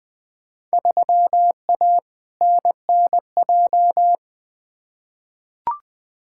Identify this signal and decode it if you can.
Morse code, heard as a keyed tone, about 20 wpm, 700 Hz. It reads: 3A NNJ
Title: Field Day Exchanges - 20wpm